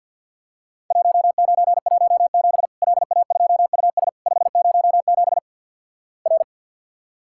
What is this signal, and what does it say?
Morse code, heard as a keyed tone, about 48 wpm, 700 Hz. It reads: 1917 LA2US 507 R
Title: Worked All Europe DX Contest - QTC Exchange - 50wpm